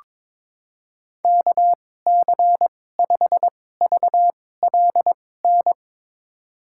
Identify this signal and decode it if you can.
Morse code, heard as a keyed tone, about 22 wpm, spaced as for 16 wpm, 700 Hz. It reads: KC5VLN
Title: USA Call Signs - 2x Character Spacing - 22wpm